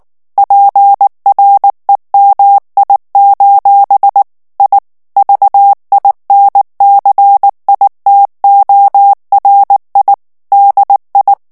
Morse code, heard as a keyed tone, about 19 wpm, 800 Hz. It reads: PREMI: I VINCITORI DI
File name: morse mp3 / ITALIEN texte clair 2 vitesse 18 mots minute